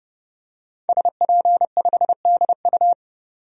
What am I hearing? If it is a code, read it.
Morse code, heard as a keyed tone, about 30 wpm, 700 Hz. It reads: SP5DU
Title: DXCC-100 Call Signs - 5 Characters - 30wpm